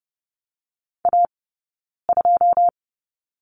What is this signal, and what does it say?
Morse code, heard as a keyed tone, about 30 wpm, 700 Hz. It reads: A 2
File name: course / 30:10wpm / 086 Lesson 15d - Number 5 - Two Characters 30wpm